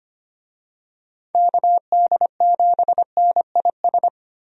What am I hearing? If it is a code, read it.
Morse code, heard as a keyed tone, about 25 wpm, 700 Hz. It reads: KD7NIS
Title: USA Call Signs - 6 Characters - Repeated 2x - 25wpm